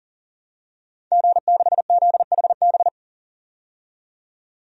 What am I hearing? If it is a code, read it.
Morse code, heard as a keyed tone, about 40 wpm, 700 Hz. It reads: G6ZHB
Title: DXCC-100 Call Signs - 5 Characters - 40wpm